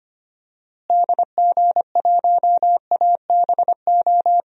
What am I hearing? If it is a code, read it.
Morse code, heard as a keyed tone, about 25 wpm, 700 Hz. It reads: DG1ABO
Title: DXCC-100 Call Signs - 6 Characters - Repeated 3x - 25wpm